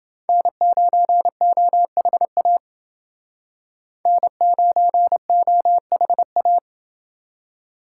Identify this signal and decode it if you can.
Morse code, heard as a keyed tone, about 28 wpm, 700 Hz. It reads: N9OHA N9OHA